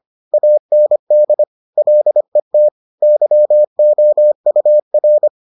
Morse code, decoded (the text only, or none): AND LET YOUR